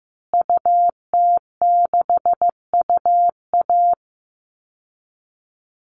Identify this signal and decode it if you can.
Morse code, heard as a keyed tone, about 15 wpm, 700 Hz. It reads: UT6UA
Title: Call Signs Repeated 3x - 15wpm